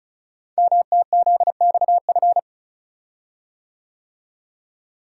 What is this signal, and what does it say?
Morse code, heard as a keyed tone, about 36 wpm, 700 Hz. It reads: MTZXF
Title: Code Groups - 5 Letters - 35wpm